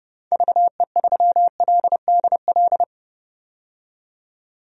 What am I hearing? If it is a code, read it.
Morse code, heard as a keyed tone, about 31 wpm, 690 Hz. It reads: VE3LDL
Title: DXCC-100 Call Signs - 6 Characters - 30wpm